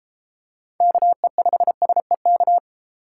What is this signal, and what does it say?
Morse code, heard as a keyed tone, about 33 wpm, 700 Hz. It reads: KE5SEK